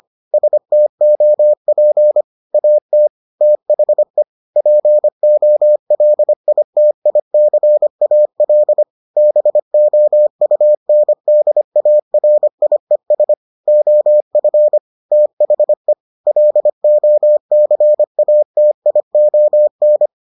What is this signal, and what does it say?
Morse code, heard as a keyed tone, about 25 wpm, 600 Hz. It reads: STOP AT THE POLITICAL BOUNDARIES OF THE LOCATION